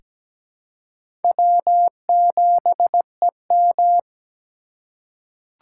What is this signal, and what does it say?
Morse code, heard as a keyed tone, about 17 wpm, 700 Hz. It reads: W7EM
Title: USA Call Signs - 2x Character Spacing - 17wpm